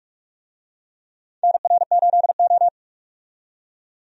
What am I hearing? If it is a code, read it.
Morse code, heard as a keyed tone, about 44 wpm, 700 Hz. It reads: NR8O
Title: USA Call Signs - 4 Characters - 45wpm